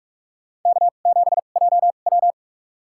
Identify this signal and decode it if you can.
Morse code, heard as a keyed tone, about 47 wpm, spaced as for 33 wpm, 700 Hz. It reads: K7JW
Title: USA Call Signs - 2x Character Spacing - 45wpm